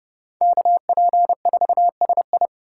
Morse code, heard as a keyed tone, about 32 wpm, 700 Hz. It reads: KP4SI
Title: USA Call Signs - 3x Character Spacing - 30wpm